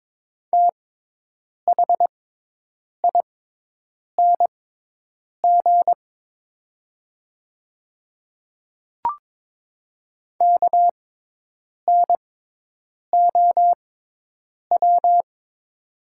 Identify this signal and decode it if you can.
Morse code, heard as a keyed tone, about 22 wpm, spaced as for 8 wpm, 700 Hz. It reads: THING KNOW
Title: 6x Character Spacing - Top 100 Words - 22wpm